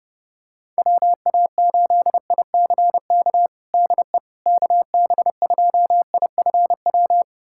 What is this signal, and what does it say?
Morse code, heard as a keyed tone, about 30 wpm, 700 Hz. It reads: WA8ICK DE KB2IFW
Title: USA Callsign DE Callsign 30wpm